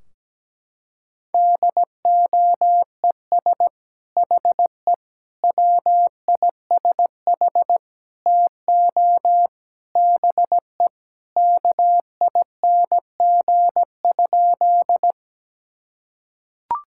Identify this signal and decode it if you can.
Morse code, heard as a keyed tone, about 17 wpm, 700 Hz. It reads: DOES HE WISH TO BE KING?
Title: Sentences from Top 1600 Words 17wpm